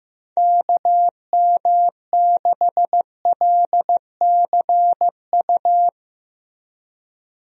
Morse code, heard as a keyed tone, about 15 wpm, 700 Hz. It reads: KM6LCU
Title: USA Call Signs - 6 Characters - Repeated 3x - 15wpm